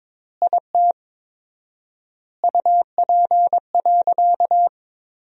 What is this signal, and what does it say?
Morse code, heard as a keyed tone, about 22 wpm, 700 Hz. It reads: IT UP.
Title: 4x Word-Spacing - Sentences from Top 100 Words - 22wpm